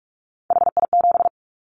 Morse code, heard as a keyed tone, about 44 wpm, 700 Hz. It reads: HI7
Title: Callsign Prefixes for Top 100 DXCC - 45wpm